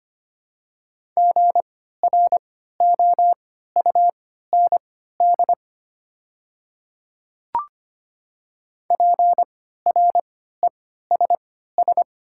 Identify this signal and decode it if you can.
Morse code, heard as a keyed tone, about 25 wpm, spaced as for 14 wpm, 690 Hz. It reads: GROUND PRESS
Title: 3x Character Spacing - Top 500 Words - 25wpm